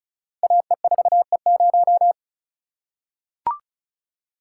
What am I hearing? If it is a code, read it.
Morse code, heard as a keyed tone, about 36 wpm, 700 Hz. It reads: AE4E0
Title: Code Groups - 5 Letter:Numbers - 35wpm